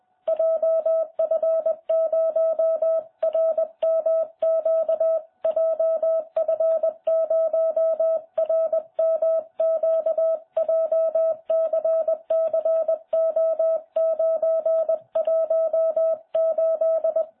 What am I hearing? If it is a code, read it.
Morse code, heard as a keyed tone, about 20 wpm, 660 Hz. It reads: JF0RMQ JF0RMQ JCCO918